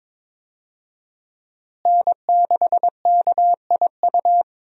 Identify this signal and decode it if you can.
Morse code, heard as a keyed tone, about 22 wpm, 700 Hz. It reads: N6KIU
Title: USA Call Signs - 5 Characters - Repeated 3x - 22wpm